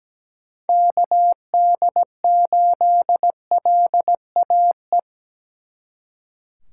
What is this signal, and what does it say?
Morse code, heard as a keyed tone, about 20 wpm, 700 Hz. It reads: KD8LAE
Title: Speed Racing - USA 2x3 Call Signs 17wpm